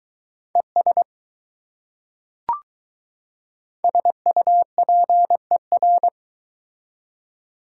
Morse code, heard as a keyed tone, about 20 wpm, 700 Hz. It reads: ES SUPER